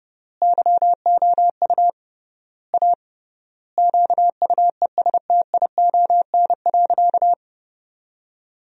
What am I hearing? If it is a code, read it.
Morse code, heard as a keyed tone, about 30 wpm, 700 Hz. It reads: YOU A QUESTION.
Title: 3x Word-Spacing - Sentences from Top 400 Words - 30wpm